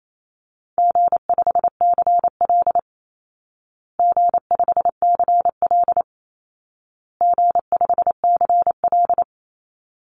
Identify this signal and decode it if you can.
Morse code, heard as a keyed tone, about 28 wpm, 700 Hz. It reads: G5CL G5CL G5CL
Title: Easy Call Signs Repeated 3x - 28wpm